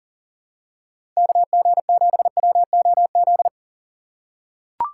Morse code, heard as a keyed tone, about 40 wpm, 700 Hz. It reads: KG7WOZ
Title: USA Call Signs - 2x Character Spacing - 40wpm